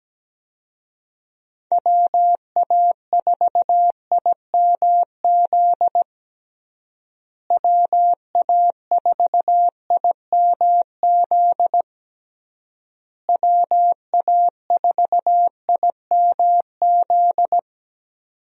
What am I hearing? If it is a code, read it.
Morse code, heard as a keyed tone, about 17 wpm, 700 Hz. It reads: WA4IMZ WA4IMZ WA4IMZ
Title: USA Call Signs - 6 Characters - Repeated 3x - 17wpm